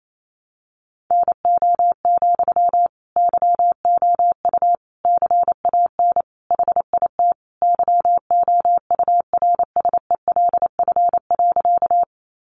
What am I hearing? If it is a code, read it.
Morse code, heard as a keyed tone, about 28 wpm, 700 Hz. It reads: NO, YOU CAN HIT YOURSELF.
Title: Sentences from Top 1200 Words 28wpm